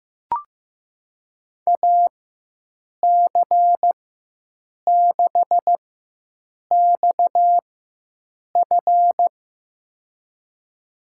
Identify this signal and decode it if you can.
Morse code, heard as a keyed tone, about 15 wpm, spaced as for 7 wpm, 700 Hz. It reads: AC6XF
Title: USA Call Signs - 5 Characters - 4x Character Spacing - 15wpm